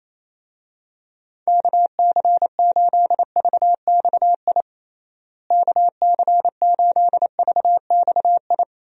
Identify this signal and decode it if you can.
Morse code, heard as a keyed tone, about 27 wpm, 700 Hz. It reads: KC8VXI KC8VXI
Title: USA Call Signs - 6 Characters - Repeated 3x - 28wpm